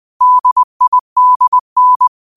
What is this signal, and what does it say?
Morse code, heard as a keyed tone, about 20 wpm, 1000 Hz. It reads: DIDN
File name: qotdfiles / 20wpm